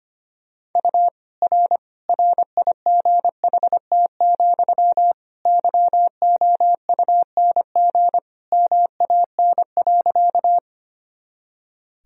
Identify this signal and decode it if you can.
Morse code, heard as a keyed tone, about 25 wpm, 700 Hz. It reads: U R RIGHT, YOUNG MAN.
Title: Sentences from Top 400 Words and Abbreviations 25wpm